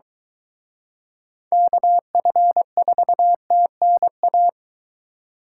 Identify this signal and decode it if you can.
Morse code, heard as a keyed tone, about 23 wpm, 700 Hz. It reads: KF4TNA